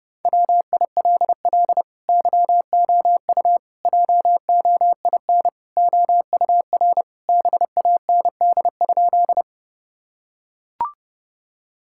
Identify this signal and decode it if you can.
Morse code, heard as a keyed tone, about 30 wpm, 700 Hz. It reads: WILL YOU JOIN OUR BAND?
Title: Sentences from Top 1600 Words 30wpm